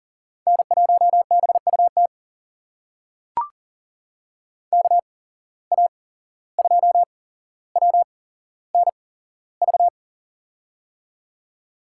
Morse code, heard as a keyed tone, about 38 wpm, spaced as for 16 wpm, 700 Hz. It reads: N1BUT KA2WNV
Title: USA Call Signs - 8x Character Spacing - 40wpm